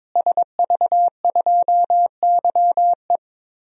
Morse code, heard as a keyed tone, about 21 wpm, 700 Hz. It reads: SV2YE
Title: DXCC-100 Call Signs - 5 Characters - Repeated 3x - 22wpm